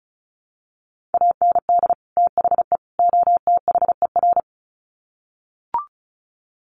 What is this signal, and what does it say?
Morse code, heard as a keyed tone, about 34 wpm, 700 Hz. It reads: AND THE OTHER
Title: Sets of 3 Words 35wpm